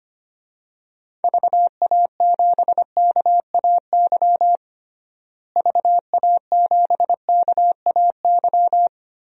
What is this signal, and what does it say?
Morse code, heard as a keyed tone, about 25 wpm, 700 Hz. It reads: VA7KAY VA7KAY
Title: DXCC-100 Call Signs - 6 Characters - Repeated 2x - 25wpm